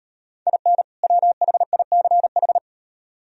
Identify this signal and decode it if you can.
Morse code, heard as a keyed tone, about 36 wpm, 700 Hz. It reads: IN WHICH